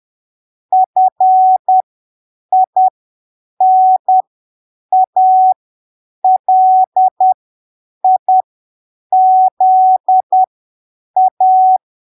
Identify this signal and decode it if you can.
Morse code, heard as a keyed tone, about 10 wpm, spaced as for 7 wpm, 750 Hz. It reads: FINALIZA
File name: arquivos / texto / texto 27 10ppm